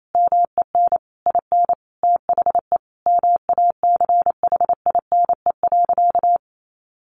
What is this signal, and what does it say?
Morse code, heard as a keyed tone, about 28 wpm, 700 Hz. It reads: MEN IN THE MACHINE.
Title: A Sound Of Thunder 28wpm